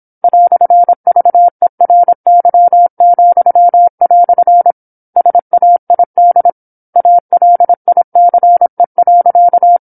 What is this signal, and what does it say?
Morse code, heard as a keyed tone, about 27 wpm, 700 Hz. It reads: "VERY," SAID ALICE.
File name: 08-Chapter-08 26 WPM